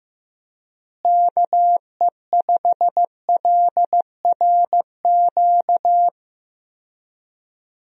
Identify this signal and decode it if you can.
Morse code, heard as a keyed tone, about 15 wpm, 700 Hz. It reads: KE5LRQ
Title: USA Call Signs - 6 Characters - Repeated 2x - 15wpm